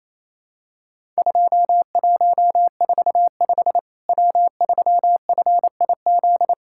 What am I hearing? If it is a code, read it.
Morse code, heard as a keyed tone, about 28 wpm, 700 Hz. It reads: 2145 W3FIZ
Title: Worked All Europe DX Contest - QTC Exchange - 28wpm